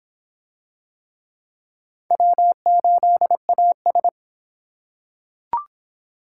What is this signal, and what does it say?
Morse code, heard as a keyed tone, about 23 wpm, 700 Hz. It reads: W8AS